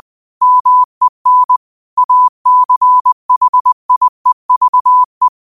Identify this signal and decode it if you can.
Morse code, heard as a keyed tone, about 20 wpm, 1000 Hz. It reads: MEN ACHIEVE
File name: qotdfiles / 20wpm